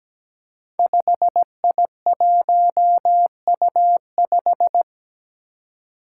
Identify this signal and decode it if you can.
Morse code, heard as a keyed tone, about 17 wpm, 700 Hz. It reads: 5I1U5
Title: Five Challenging Letter-Number Combinations 17wpm